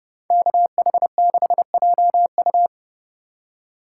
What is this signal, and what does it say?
Morse code, heard as a keyed tone, about 29 wpm, 700 Hz. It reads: KH6JU